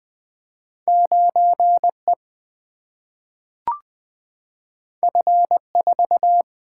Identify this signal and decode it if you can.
Morse code, heard as a keyed tone, about 20 wpm, 700 Hz. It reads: 9E F4
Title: Single-Letter Single-Number 20wpm